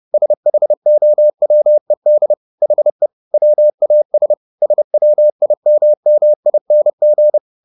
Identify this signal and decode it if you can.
Morse code, heard as a keyed tone, about 30 wpm, 600 Hz. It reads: SHOWED HE WAS SWIMMING
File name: the old man and the sea / the old man and the sea 1 30wpm